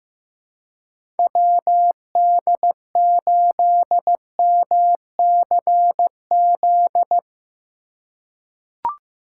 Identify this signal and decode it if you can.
Morse code, heard as a keyed tone, about 15 wpm, 700 Hz. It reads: WD8MCZ
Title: USA Call Signs - 6 Characters - 4x Character Spacing - 15wpm